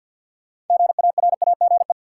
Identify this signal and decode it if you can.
Morse code, heard as a keyed tone, about 52 wpm, 700 Hz. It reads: GARAGE
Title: Sentences from Top 1600 Words - New Word Drill Alt - 50wpm